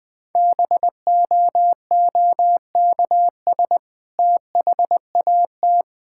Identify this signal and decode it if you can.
Morse code, heard as a keyed tone, about 20 wpm, 700 Hz. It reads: BOOKS THAT
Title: Sentences with 2-Word Phrases - 20wpm